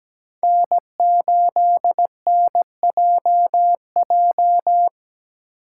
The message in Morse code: N8NJJ

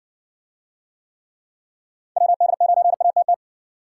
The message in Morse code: WD9ATT

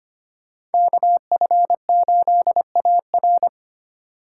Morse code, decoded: KF8AR

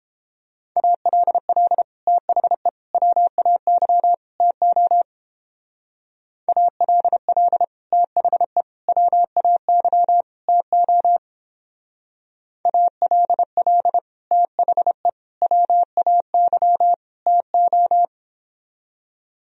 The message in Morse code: ALL THE WAY TO ALL THE WAY TO ALL THE WAY TO

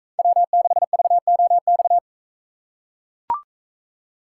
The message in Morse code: W6VOX

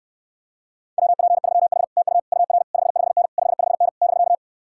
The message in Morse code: WP3H TU UR 55N 55N <BK>

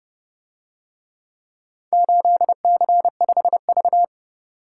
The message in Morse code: 8C5V

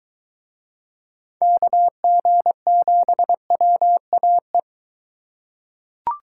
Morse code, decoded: KG7WAE

